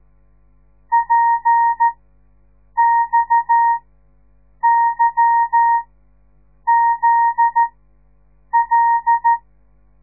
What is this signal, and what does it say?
Morse code, heard as a keyed tone, about 13 wpm, spaced as for 7 wpm, 930 Hz. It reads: PXYZL
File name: radiomorningstar / morse / les 9